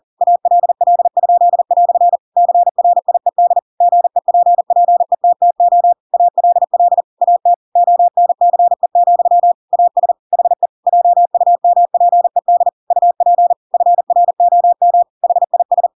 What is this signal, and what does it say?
Morse code, heard as a keyed tone, about 40 wpm, 700 Hz. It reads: ALL?" CRIED GEPPETTO ALL AT ONCE, AS HE JUMPED UP FROM HIS
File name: Pinocchio / Pinocchio ch8 40 40 ik1hge